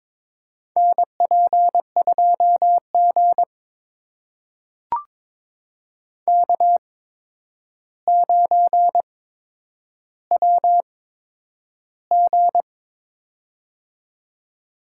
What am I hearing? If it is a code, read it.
Morse code, heard as a keyed tone, about 22 wpm, spaced as for 9 wpm, 700 Hz. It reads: NP2G K9WG
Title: USA Call Signs - 4 Characters - 8x Character Spacing - 22wpm